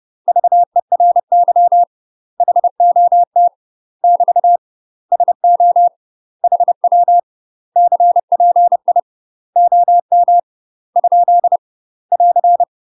Morse code, heard as a keyed tone, about 30 wpm, 700 Hz. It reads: VERY HOT <BT> SO HW CPI OM ? <AR>
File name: UK MASTER QSOs TRAINING / 30WPM / UK MASTER QSOs 0049